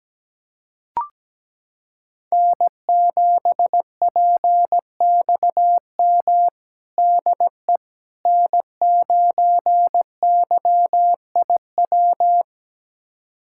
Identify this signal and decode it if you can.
Morse code, heard as a keyed tone, about 17 wpm, 700 Hz. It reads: N7PXM DE N9YIW
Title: USA Callsign DE Callsign - 5 Characters - 17wpm